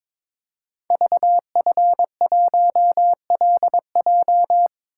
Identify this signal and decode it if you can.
Morse code, heard as a keyed tone, about 22 wpm, 700 Hz. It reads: VF1LJ
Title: Code Groups - 5 Letter:Numbers - 22wpm